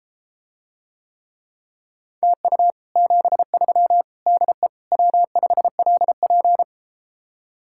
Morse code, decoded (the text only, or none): TU 73 DE W5LP